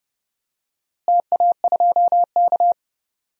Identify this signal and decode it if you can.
Morse code, heard as a keyed tone, about 30 wpm, 700 Hz. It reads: TA2K
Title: DXCC-100 Call Signs - 4 Characters - Repeated 2x - 30wpm